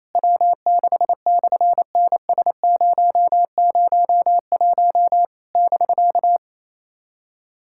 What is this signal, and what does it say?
Morse code, heard as a keyed tone, about 28 wpm, 700 Hz. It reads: W6/NS001 <BK>
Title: SOTA QSO Practice 28wpm